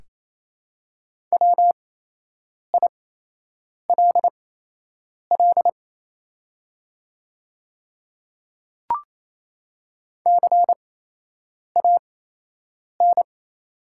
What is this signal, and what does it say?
Morse code, heard as a keyed tone, about 28 wpm, spaced as for 8 wpm, 690 Hz. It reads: WILL CAN
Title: 8x Character Spacing - Top 100 Words - 28wpm